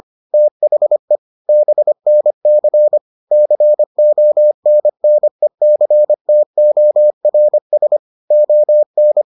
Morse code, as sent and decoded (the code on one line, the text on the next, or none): - .... / -... -. -.-. / -.-. --- -. -. -.-. - --- .-. ... / --- -.
THE BNC CONNECTORS ON